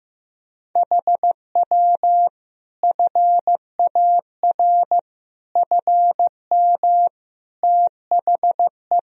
Morse code, heard as a keyed tone, about 16 wpm, 700 Hz. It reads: HW FAR FM THE
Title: Sentences from Top 400 Words and Abbreviations 15wpm